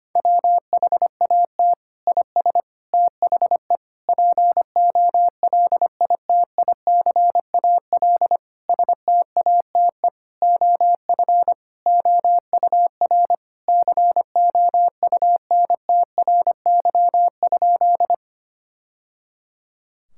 Morse code, heard as a keyed tone, about 25 wpm, 700 Hz. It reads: WHAT IS THE POLITICAL STATE OF OUR COUNTRY?
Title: Sentences from Top 1000 Words - Review Part 5 - 25wpm